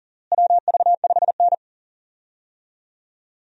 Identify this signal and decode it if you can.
Morse code, heard as a keyed tone, about 40 wpm, 700 Hz. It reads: WV5N
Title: USA Call Signs - 4 Characters - Repeated 3x - 40wpm